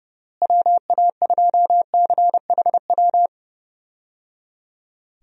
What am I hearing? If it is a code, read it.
Morse code, heard as a keyed tone, about 30 wpm, 700 Hz. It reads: WA2CHW
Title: USA Call Signs - 6 Characters - Repeated 3x - 30wpm